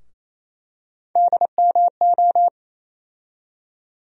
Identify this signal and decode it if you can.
Morse code, heard as a keyed tone, about 28 wpm, 700 Hz. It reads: DMO
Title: USA Call Sign Postfixes - 3 Letters 28wpm